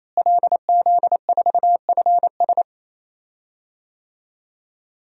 Morse code, heard as a keyed tone, about 28 wpm, 700 Hz. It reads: LZ4FS
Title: DXCC-100 Call Signs - 5 Characters - Repeated 3x - 28wpm